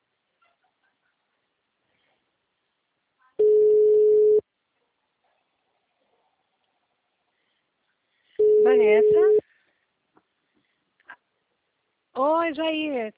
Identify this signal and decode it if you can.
Ringback tone (Europe, 425 Hz), with speech.